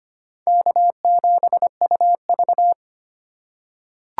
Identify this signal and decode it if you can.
Morse code, heard as a keyed tone, about 25 wpm, 700 Hz. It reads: K7UV
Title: USA Call Signs - 4 Characters - 4x Character Spacing - 25wpm